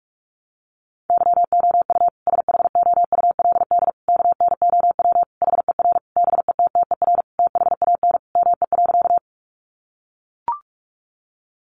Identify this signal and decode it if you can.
Morse code, heard as a keyed tone, about 44 wpm, 700 Hz. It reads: YOU SHOULD KNOW HER BETTER THAN ME.
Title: Sentences from Top 400 Words 45wpm